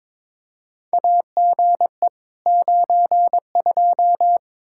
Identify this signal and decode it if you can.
Morse code, heard as a keyed tone, about 22 wpm, 700 Hz. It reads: AGE 92
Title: Practice QSO - Exchange 3 - More Info - 22wpm